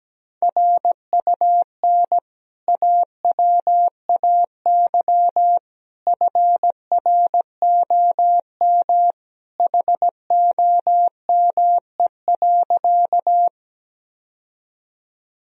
Morse code, heard as a keyed tone, about 17 wpm, 700 Hz. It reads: RUN AWAY FROM HOME.
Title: Sentences from Top 1000 Words - Review Part 1 - 17wpm